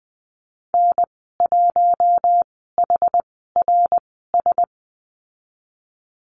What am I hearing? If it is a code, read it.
Morse code, heard as a keyed tone, about 20 wpm, spaced as for 15 wpm, 700 Hz. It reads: N1HRS
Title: 076 Lesson 13g - Letter L - Call Signs 20wpm